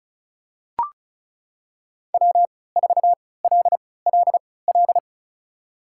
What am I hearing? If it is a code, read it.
Morse code, heard as a keyed tone, about 35 wpm, spaced as for 20 wpm, 700 Hz. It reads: W4LLL
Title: USA Call Signs - 5 Characters - 3x Character Spacing - 35wpm